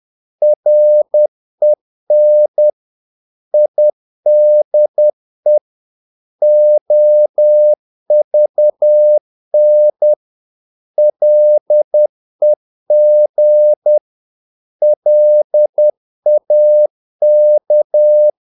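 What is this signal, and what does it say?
Morse code, heard as a keyed tone, about 10 wpm, 600 Hz. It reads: REN IDE OVN LEG LAK
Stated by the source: Morse practice of short Danish words at 10 wpm, from CW med Gnister.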